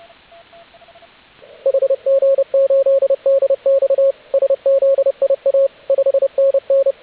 Morse code, heard as a keyed tone, about 30 wpm, 540 Hz, with radio noise. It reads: HG8DX SZIA 5NN